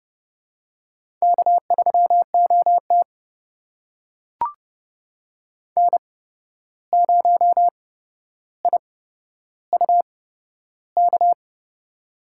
Morse code, heard as a keyed tone, about 30 wpm, spaced as for 12 wpm, 700 Hz. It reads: K3OT N0IUK
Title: USA Call Signs - 8x Character Spacing - 30wpm